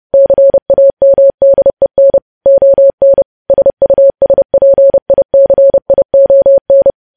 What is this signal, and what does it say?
Morse code, heard as a keyed tone, about 30 wpm, 550 Hz. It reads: CAMDEN ON SUSPICION
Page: news headlines in morse code at 30 wpm